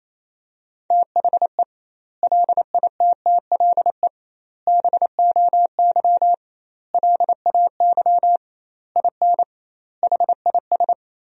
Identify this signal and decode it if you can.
Morse code, heard as a keyed tone, about 28 wpm, 700 Hz. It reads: THE LITTLE BOY LAY IN HIS
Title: 2x Word-Spacing - Sentences from Top 500 Words - 28wpm